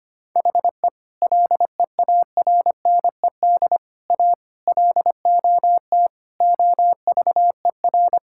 Morse code, heard as a keyed tone, about 25 wpm, 700 Hz. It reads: HE LEARNED A LOT OVER
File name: practice / 25wpm / Sentences with 3-Word Phrases - 25wpm